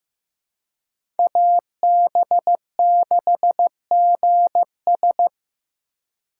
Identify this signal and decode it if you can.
Morse code, heard as a keyed tone, about 15 wpm, 700 Hz. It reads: AB6GS
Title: USA Call Signs - 5 Characters - 3x Character Spacing - 15wpm